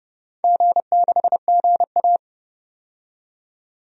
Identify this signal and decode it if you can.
Morse code, heard as a keyed tone, about 31 wpm, 700 Hz. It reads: G6GA
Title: DXCC-100 Call Signs - 4 Characters - 30wpm